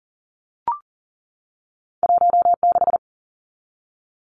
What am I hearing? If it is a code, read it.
Morse code, heard as a keyed tone, about 41 wpm, 700 Hz. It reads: 16
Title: Numbers - 2-Digit 40wpm